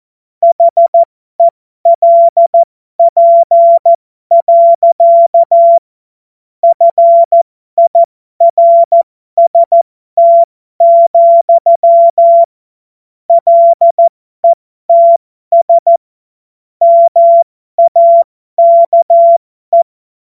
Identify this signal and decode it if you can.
Morse code, heard as a keyed tone, about 14 wpm, spaced as for 12 wpm, 680 Hz. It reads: HELP. FIRST, LETS MAKE
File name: Morsekurs / Witze